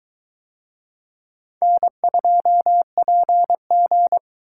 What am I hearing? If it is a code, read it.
Morse code, heard as a keyed tone, about 22 wpm, 700 Hz. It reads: N2PG